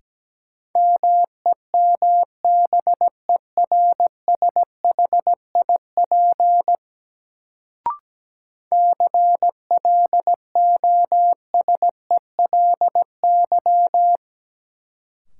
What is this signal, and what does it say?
Morse code, heard as a keyed tone, about 17 wpm, 700 Hz. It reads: MEMBERSHIP CLOSELY
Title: Sentences from Top 1400 Words - New Word Drill Alt - 17wpm